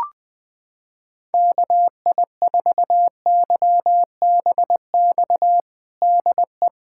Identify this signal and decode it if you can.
Morse code, heard as a keyed tone, about 21 wpm, 700 Hz. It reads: KI4YBX DE
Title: USA Callsign DE Callsign - 6 Characters - 20wpm